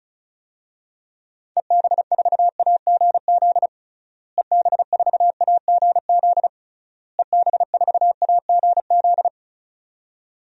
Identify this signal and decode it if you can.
Morse code, heard as a keyed tone, about 36 wpm, 700 Hz. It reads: EB4AGZ EB4AGZ EB4AGZ
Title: DXCC-100 Call Signs - 6 Characters - Repeated 3x - 35wpm